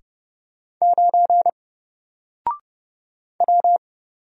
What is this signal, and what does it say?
Morse code, heard as a keyed tone, about 30 wpm, 700 Hz. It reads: 9 W
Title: Single Letter-Number Alt 30wpm